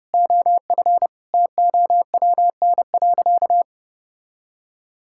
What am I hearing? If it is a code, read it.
Morse code, heard as a keyed tone, about 28 wpm, 700 Hz. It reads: OF TOWN.
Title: Sentences from Top 1000 Words - Review Part 6 - 30wpm